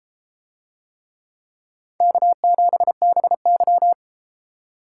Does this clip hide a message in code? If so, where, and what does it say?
Yes: Morse code from 2.0 to 3.9 s, heard as a keyed tone, about 33 wpm, 700 Hz. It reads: K7BY